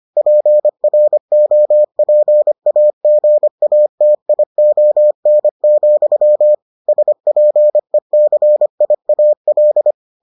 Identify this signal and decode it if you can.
Morse code, heard as a keyed tone, about 25 wpm, 600 Hz. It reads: PROPAGATION, SPECIAL